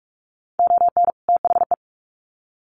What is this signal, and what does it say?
Morse code, heard as a keyed tone, about 44 wpm, 700 Hz. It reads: ON THE
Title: Sets of 2 Words 45wpm